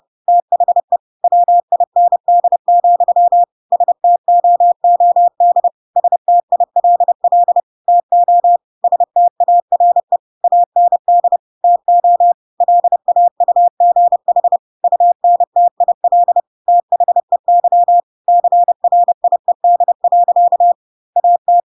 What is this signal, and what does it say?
Morse code, heard as a keyed tone, about 30 wpm, 700 Hz. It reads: THE WIND, STOOD STILL TO STARE AND TO LAUGH UNTIL THEY CRIED. AT
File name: Pinocchio / Pinocchio ch3 30 30 ik1hge